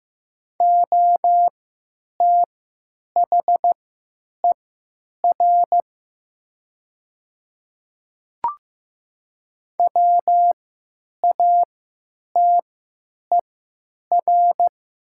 Morse code, heard as a keyed tone, about 15 wpm, spaced as for 9 wpm, 700 Hz. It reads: OTHER WATER
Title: 3x Character Spacing - Top 100 Words - 15wpm